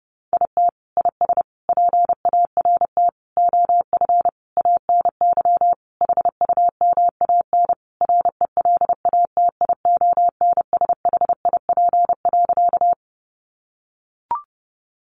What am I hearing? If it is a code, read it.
Morse code, heard as a keyed tone, about 30 wpm, 700 Hz. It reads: IT IS PART OF ANY HUMAN RELATIONSHIP.
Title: Sentences from Top 1100 Words 30wpm